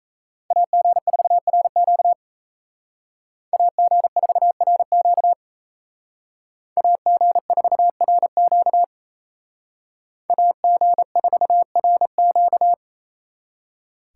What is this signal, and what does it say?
Morse code, heard as a keyed tone, about 36 wpm, 700 Hz. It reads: AG4RQ AG4RQ AG4RQ AG4RQ